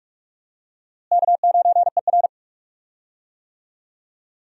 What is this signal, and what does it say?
Morse code, heard as a keyed tone, about 46 wpm, 700 Hz. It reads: K9ER
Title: USA Call Signs - 4 Characters - Repeated 3x - 45wpm